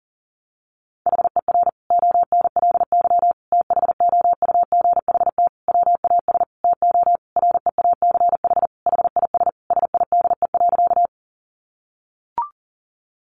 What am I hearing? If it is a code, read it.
Morse code, heard as a keyed tone, about 40 wpm, 700 Hz. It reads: HER ONLY THOUGHT WAS TO REACH HIS SIDE.
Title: Sentences from Top 400 Words 40wpm